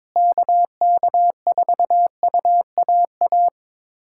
Morse code, heard as a keyed tone, about 22 wpm, 700 Hz. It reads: KK4UAA
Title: USA Callsign DE Callsign - 6 Characters - 22wpm